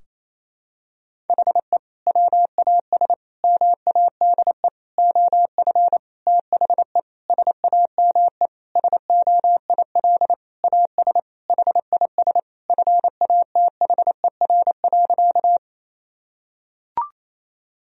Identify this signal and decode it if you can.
Morse code, heard as a keyed tone, about 28 wpm, 700 Hz. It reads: HE WAS MADE OF THE SAME SOIL AS HIS FATHER.
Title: Sentences from Top 1000 Words - Review Part 4 - 28wpm